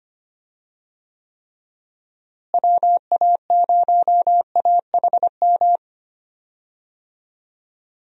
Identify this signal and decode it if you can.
Morse code, heard as a keyed tone, about 25 wpm, 700 Hz. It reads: WA0AHM
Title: USA Call Signs - 6 Characters - Repeated 3x - 25wpm